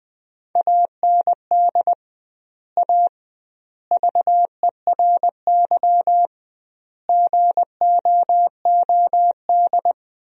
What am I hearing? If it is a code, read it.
Morse code, heard as a keyed tone, about 20 wpm, 700 Hz. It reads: AND A VERY GOOD
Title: 2x Word-Spacing - Sentences from Top 200 Words - 20wpm